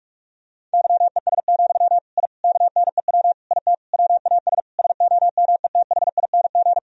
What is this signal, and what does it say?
Morse code, heard as a keyed tone, about 46 wpm, 700 Hz. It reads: YES, I KNEW IT WAS SOMETHING
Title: Sentences from Top 1000 Words - Review Part 2 - 45wpm